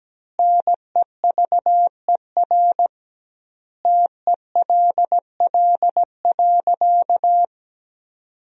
Morse code, heard as a keyed tone, about 17 wpm, 700 Hz. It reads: NEVER TELL.
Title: 2x Word-Spacing - Sentences from Top 300 Words - 17wpm